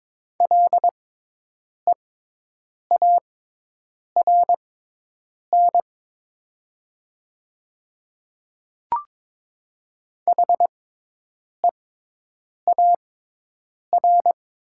Morse code, heard as a keyed tone, about 22 wpm, spaced as for 8 wpm, 700 Hz. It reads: LEARN HEAR
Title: 6x Character Spacing - Top 300 Words - 22wpm